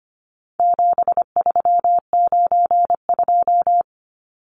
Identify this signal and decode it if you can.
Morse code, heard as a keyed tone, about 25 wpm, 700 Hz. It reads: 7392
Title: Numbers - 4-Digit 25wpm